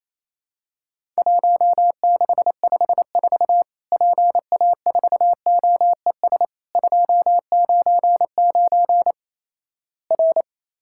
Morse code, heard as a keyed tone, about 28 wpm, 700 Hz. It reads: 1654 PA4OES 299 R
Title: Worked All Europe DX Contest - QTC Exchange - 28wpm